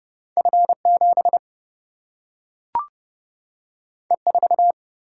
Morse code, heard as a keyed tone, about 31 wpm, 700 Hz. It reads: F7 E4
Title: Single-Letter Single-Number 30wpm